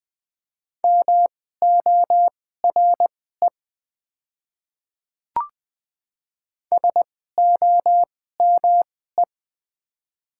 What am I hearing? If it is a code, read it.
Morse code, heard as a keyed tone, about 20 wpm, spaced as for 15 wpm, 700 Hz. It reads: MORE SOME
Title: 2x Character Spacing - Top 100 Words - 20wpm